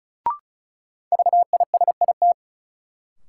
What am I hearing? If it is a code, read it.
Morse code, heard as a keyed tone, about 36 wpm, 700 Hz. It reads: VISIT
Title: Sentences from Top 900 Words - New Word Drill - 35wpm